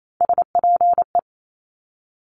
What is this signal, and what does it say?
Morse code, heard as a keyed tone, about 28 wpm, 700 Hz. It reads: SPE
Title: Top Three Letter Combinations in Words 28wpm